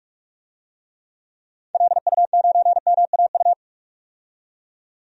Morse code, heard as a keyed tone, about 46 wpm, 700 Hz. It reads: LU9MAU
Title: DXCC-100 Call Signs - 6 Characters - 45wpm